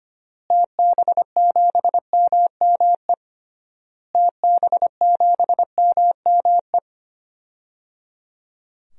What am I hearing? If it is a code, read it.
Morse code, heard as a keyed tone, about 25 wpm, 690 Hz. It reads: TB7MME TB7MME
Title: DXCC-100 Call Signs - 6 Characters - Repeated 3x - 25wpm